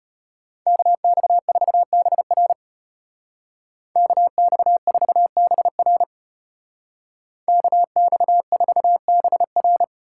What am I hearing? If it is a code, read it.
Morse code, heard as a keyed tone, about 33 wpm, 700 Hz. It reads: KX4BR KX4BR KX4BR